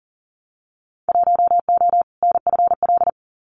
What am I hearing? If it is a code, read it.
Morse code, heard as a keyed tone, about 39 wpm, 700 Hz. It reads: 1O NFL
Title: Winter Field Day Exchanges 40wpm